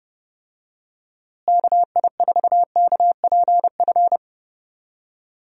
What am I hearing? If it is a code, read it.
Morse code, heard as a keyed tone, about 30 wpm, 700 Hz. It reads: KI4KPF
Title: USA Call Signs - 6 Characters - Repeated 3x - 30wpm